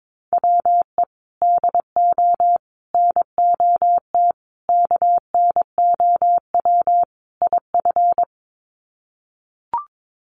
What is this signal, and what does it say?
Morse code, heard as a keyed tone, about 22 wpm, 700 Hz. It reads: WE DO NOT KNOW IF
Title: Sets of 5 Words Encore2 22wpm